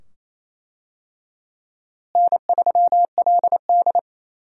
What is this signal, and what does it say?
Morse code, heard as a keyed tone, about 28 wpm, 700 Hz. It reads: N3LD
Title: USA Call Signs - 4 Characters - Repeated 2x - 28wpm